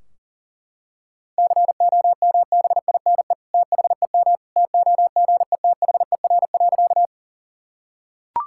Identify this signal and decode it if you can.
Morse code, heard as a keyed tone, about 40 wpm, 700 Hz. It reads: COMBINE THEM TOGETHER.
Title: Sentences from Top 1600 Words 40wpm